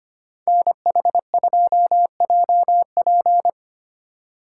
Morse code, heard as a keyed tone, about 29 wpm, 700 Hz. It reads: NH2JP
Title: Speed Racing - USA 2x2 Call Signs 25wpm